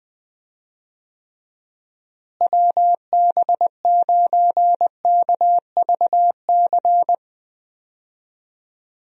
Morse code, heard as a keyed tone, about 20 wpm, 700 Hz. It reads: WB9KVC
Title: USA Call Signs - 6 Characters - Repeated 3x - 20wpm